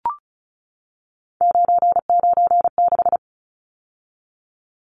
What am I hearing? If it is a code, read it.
Morse code, heard as a keyed tone, about 34 wpm, 700 Hz. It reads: 996
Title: Numbers - 3-Digit 35wpm